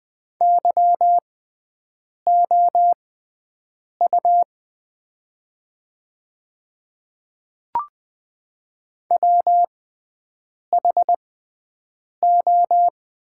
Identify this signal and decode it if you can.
Morse code, heard as a keyed tone, about 20 wpm, spaced as for 7 wpm, 700 Hz. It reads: YOU WHO
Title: 6x Character Spacing - Top 100 Words - 20wpm